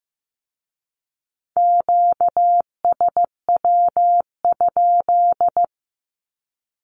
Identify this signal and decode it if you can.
Morse code, heard as a keyed tone, about 15 wpm, 700 Hz. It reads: QSW?
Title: Q-Codes with Spoken Meaning - 15wpm